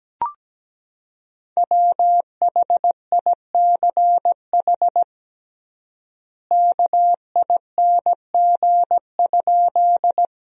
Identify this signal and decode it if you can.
Morse code, heard as a keyed tone, about 17 wpm, 700 Hz. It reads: WHICH KING?
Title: 3x Word-Spacing - Sentences from Top 400 Words - 17wpm